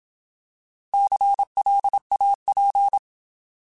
Morse code, heard as a keyed tone, about 27 wpm, 800 Hz. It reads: CLAP
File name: Alphabetical / Alphabetical mp3s / Cs @25